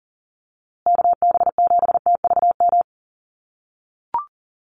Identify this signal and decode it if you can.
Morse code, heard as a keyed tone, about 40 wpm, 700 Hz. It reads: KB7TVM
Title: USA Call Signs Repeated 3x 40wpm